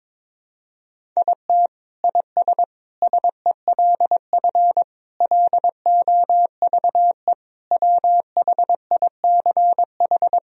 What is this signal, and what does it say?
Morse code, heard as a keyed tone, about 22 wpm, 700 Hz. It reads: IT IS SELF LOVE WHICH
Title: Sentences from Top 1000 Words - Review Part 1 - 22wpm